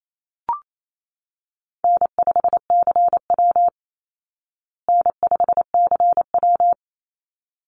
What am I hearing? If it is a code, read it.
Morse code, heard as a keyed tone, about 28 wpm, 690 Hz. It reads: N5CW N5CW
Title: Call Signs Repeated 3x - 28wpm